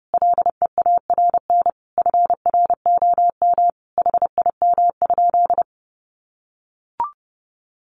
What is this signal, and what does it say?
Morse code, heard as a keyed tone, about 30 wpm, 700 Hz. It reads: LEARN FROM HIM?
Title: Sentences from Top 300 Words 30wpm